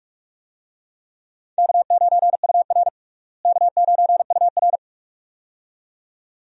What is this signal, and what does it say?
Morse code, heard as a keyed tone, about 42 wpm, 700 Hz. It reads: K9UR K9UR
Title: USA Call Signs - 4 Characters - Repeated 2x - 45wpm